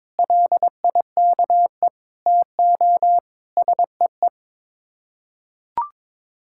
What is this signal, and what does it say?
Morse code, heard as a keyed tone, about 22 wpm, 700 Hz. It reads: LIKE TO SEE
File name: practice / 22wpm / Build A Phrase - 5-Word Phrases - 22wpm